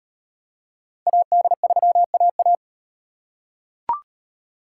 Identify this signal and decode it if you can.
Morse code, heard as a keyed tone, about 37 wpm, 700 Hz. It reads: AD3AA